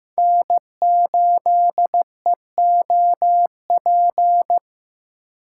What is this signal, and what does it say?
Morse code, heard as a keyed tone, about 15 wpm, 700 Hz. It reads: N8EOP
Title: USA Call Signs - 5 Characters - 8x Character Spacing - 15wpm